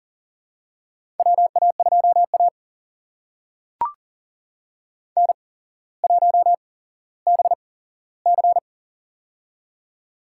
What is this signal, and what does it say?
Morse code, heard as a keyed tone, about 40 wpm, spaced as for 17 wpm, 700 Hz. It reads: WA2A N1BC
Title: USA Call Signs - 4 Characters - 8x Character Spacing - 40wpm